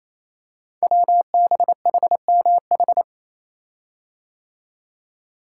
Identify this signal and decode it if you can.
Morse code, heard as a keyed tone, about 27 wpm, 700 Hz. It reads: WBHMH
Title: Code Groups - 5 Letter:Numbers - 28wpm